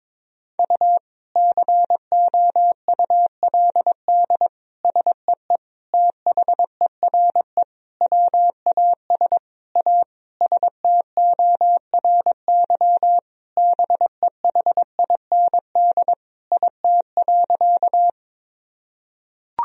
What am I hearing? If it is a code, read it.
Morse code, heard as a keyed tone, about 22 wpm, 700 Hz. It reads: U COULD SEE THERE WAS A STORY BEHIND IT.
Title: Sentences from Top 500 Words and Abbreviations 22wpm